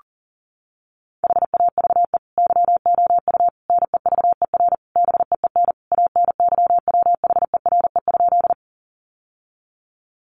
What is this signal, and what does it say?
Morse code, heard as a keyed tone, about 40 wpm, 700 Hz. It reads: HAVE YOU NEVER BEEN ANYWHERE?
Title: Sentences from Top 1400 Words 40wpm